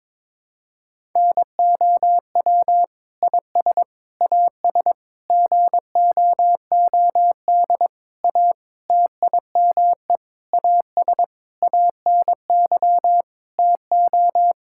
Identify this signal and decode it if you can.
Morse code, heard as a keyed tone, about 22 wpm, 700 Hz. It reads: NOW IS AS GOOD A TIME AS ANY TO
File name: Sentences with 7-Word Phrases - 22wpm